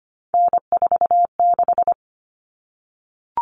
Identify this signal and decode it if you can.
Morse code, heard as a keyed tone, about 25 wpm, 700 Hz. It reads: N46
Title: Common Cut Numbers - 3-Digit 25wpm